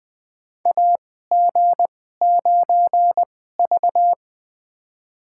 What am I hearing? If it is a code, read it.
Morse code, heard as a keyed tone, about 20 wpm, spaced as for 15 wpm, 700 Hz. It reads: AG9V
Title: USA Call Signs - 2x Character Spacing - 20wpm